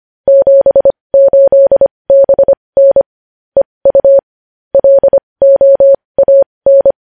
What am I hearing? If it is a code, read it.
Morse code, heard as a keyed tone, about 25 wpm, spaced as for 20 wpm, 550 Hz. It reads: 78BN EU LOAN
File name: podcast / News headlines-1x-25x20